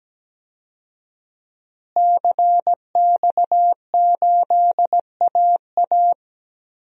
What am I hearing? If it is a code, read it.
Morse code, heard as a keyed tone, about 17 wpm, 700 Hz. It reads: CX8AA
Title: DXCC-100 Call Signs - 5 Characters - Repeated 3x - 17wpm